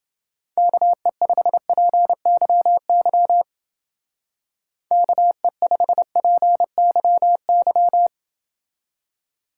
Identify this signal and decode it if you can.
Morse code, heard as a keyed tone, about 28 wpm, 700 Hz. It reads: KE5PYY KE5PYY